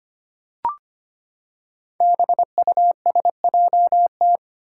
Morse code, heard as a keyed tone, about 25 wpm, 700 Hz. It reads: BUSJT
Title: Five Challenging Letter-Number Combinations 25wpm